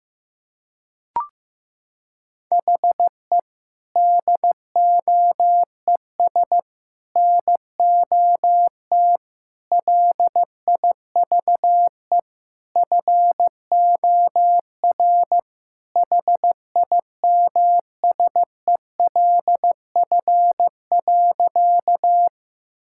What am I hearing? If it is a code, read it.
Morse code, heard as a keyed tone, about 15 wpm, 700 Hz. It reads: HE DOES NOT LIVE FOR HIMSELF.
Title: Sentences from Top 1600 Words 15wpm